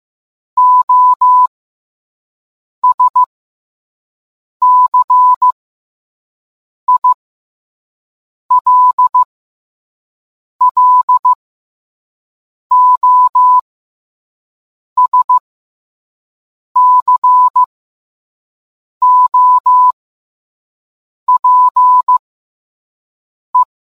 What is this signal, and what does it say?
Morse code, heard as a keyed tone, about 15 wpm, spaced as for 5 wpm, 1000 Hz. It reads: OSCILLOSCOPE